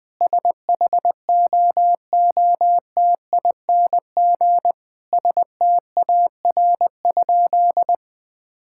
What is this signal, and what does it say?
Morse code, heard as a keyed tone, about 20 wpm, 700 Hz. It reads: SHOOTING STAR?
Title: Sentences with 8-Word Phrases - 20wpm